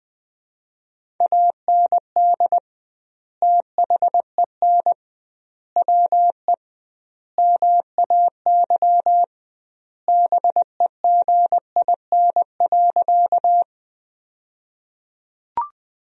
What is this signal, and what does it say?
Morse code, heard as a keyed tone, about 20 wpm, 700 Hz. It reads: AND THEN WE MAY BEGIN.
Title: 2x Word-Spacing - Sentences from Top 300 Words - 20wpm